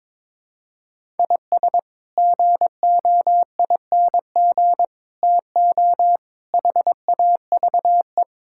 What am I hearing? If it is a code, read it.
Morse code, heard as a keyed tone, about 22 wpm, 700 Hz. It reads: IS GOING TO HAVE
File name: practice / 22wpm / Build A Phrase - 5-Word Phrases - 22wpm